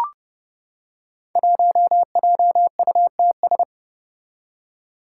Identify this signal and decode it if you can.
Morse code, heard as a keyed tone, about 30 wpm, 700 Hz. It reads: 1JUTS
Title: Five Challenging Letter-Number Combinations 30wpm